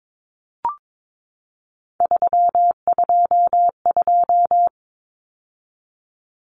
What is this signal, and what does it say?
Morse code, heard as a keyed tone, about 22 wpm, 700 Hz. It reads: 322
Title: Numbers - 3-Digit 22wpm